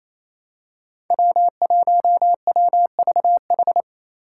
Morse code, heard as a keyed tone, about 28 wpm, 700 Hz. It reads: W1WVH
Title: Five Challenging Letter-Number Combinations 28wpm